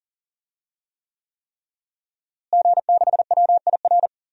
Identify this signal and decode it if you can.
Morse code, heard as a keyed tone, about 40 wpm, 700 Hz. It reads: G6WIR
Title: DXCC-100 Call Signs - 5 Characters - Repeated 3x - 40wpm